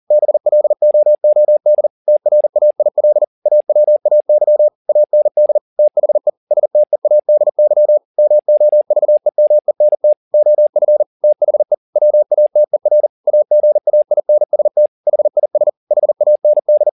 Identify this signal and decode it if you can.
Morse code, heard as a keyed tone, about 40 wpm, 600 Hz. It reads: BLOOD TRAIL AWAY AND THE STEADY MOVEMENT OF THE WATER AGAINST HIS HAND